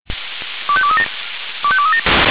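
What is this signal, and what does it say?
Five-tone selective call (ZVEI), heard as quick runs of five tones, tones of about 70 ms. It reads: 25237 25237
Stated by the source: ZVEI five-tone call received by SDR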